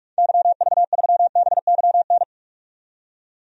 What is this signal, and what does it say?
Morse code, heard as a keyed tone, about 46 wpm, 700 Hz. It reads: YV3BYN
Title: DXCC-100 Call Signs - 6 Characters - 45wpm